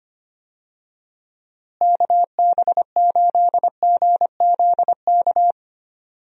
Morse code, heard as a keyed tone, about 25 wpm, 700 Hz. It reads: KB8GZK
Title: USA Call Signs - 6 Characters - Repeated 2x - 25wpm